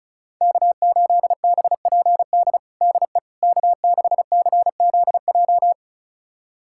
Morse code, heard as a keyed tone, about 34 wpm, 700 Hz. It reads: K8BPD DE K6CZJ
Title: USA Callsign DE Callsign - 5 Characters - 35wpm